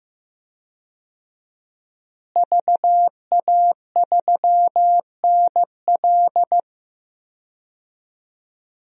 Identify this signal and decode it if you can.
Morse code, heard as a keyed tone, about 15 wpm, 700 Hz. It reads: VA3NL
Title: DXCC-100 Call Signs - 5 Characters - Repeated 3x - 15wpm